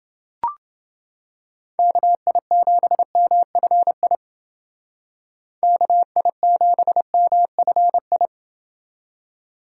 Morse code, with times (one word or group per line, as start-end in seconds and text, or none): 1.8-4.2 KI7MFI
5.6-8.3 KI7MFI